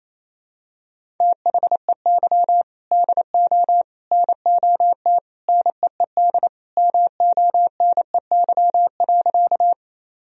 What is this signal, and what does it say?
Morse code, heard as a keyed tone, about 28 wpm, 700 Hz. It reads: THEY DO NOT NEED MONEY.
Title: Sentences from Top 1000 Words - Review Part 1 - 28wpm